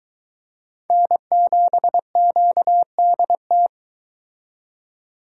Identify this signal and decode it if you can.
Morse code, heard as a keyed tone, about 22 wpm, 700 Hz. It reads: N7QDT